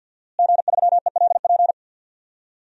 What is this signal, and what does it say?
Morse code, heard as a keyed tone, about 51 wpm, 700 Hz. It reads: G3ELP
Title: DXCC-100 Call Signs - 5 Characters - Repeated 3x - 50wpm